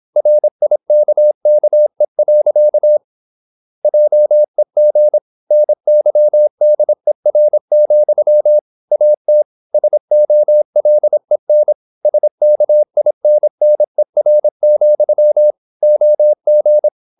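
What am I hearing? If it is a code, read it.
Morse code, heard as a keyed tone, about 26 wpm, 600 Hz. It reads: RIKKE. JEG NYDER, AT SOLEN SKINNER, OG
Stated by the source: High-speed (QRQ) Morse at 26 wpm.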